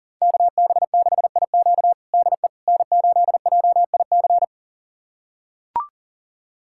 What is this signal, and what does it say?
Morse code, heard as a keyed tone, about 40 wpm, 700 Hz. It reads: KB6IQ DE N8JIC
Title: USA Callsign DE Callsign - 5 Characters - 40wpm